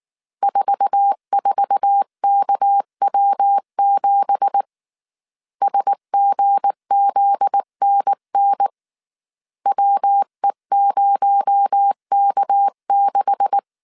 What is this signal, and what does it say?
Morse code, heard as a keyed tone, about 19 wpm, 800 Hz. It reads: 44KW7 SGZNN WE0K6
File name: CW / DET / 1 9 1500C S C 18wpm 2539